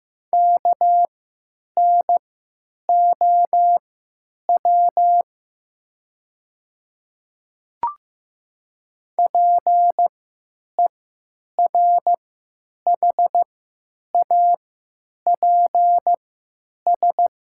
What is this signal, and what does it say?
Morse code, heard as a keyed tone, about 15 wpm, spaced as for 8 wpm, 700 Hz. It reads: KNOW PERHAPS
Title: 3x Character Spacing - Top 500 Words - 15wpm